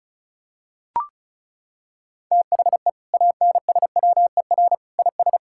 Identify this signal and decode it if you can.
Morse code, heard as a keyed tone, about 34 wpm, 700 Hz. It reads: THE ANSWER IS
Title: Sentences from Top 1600 Words 35wpm